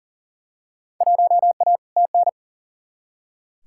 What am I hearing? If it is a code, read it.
Morse code, heard as a keyed tone, about 40 wpm, 700 Hz. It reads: 1A TN
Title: Field Day Exchanges - 40wpm